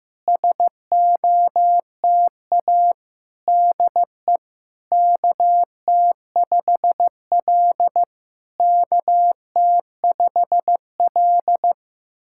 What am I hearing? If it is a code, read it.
Morse code, heard as a keyed tone, about 15 wpm, 700 Hz. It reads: SOTA DE KT5L KT5L
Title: SOTA QSO Practice 15wpm